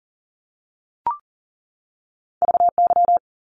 Morse code, heard as a keyed tone, about 40 wpm, 700 Hz. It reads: VY
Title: Abbreviations 40wpm